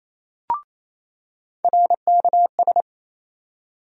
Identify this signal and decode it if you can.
Morse code, heard as a keyed tone, about 28 wpm, 700 Hz. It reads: RKS
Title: USA Call Sign Postfixes - 3 Letters 28wpm